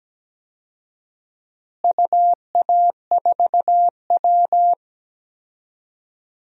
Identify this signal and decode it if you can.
Morse code, heard as a keyed tone, about 17 wpm, 700 Hz. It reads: UA4W
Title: DXCC-100 Call Signs - 4 Characters - Repeated 2x - 17wpm